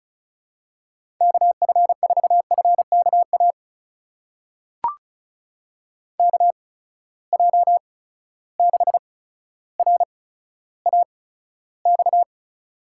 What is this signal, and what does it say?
Morse code, heard as a keyed tone, about 36 wpm, spaced as for 15 wpm, 700 Hz. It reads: KF4FKA KJ6RAX
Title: USA Call Signs - 6 Characters - 8x Character Spacing - 35wpm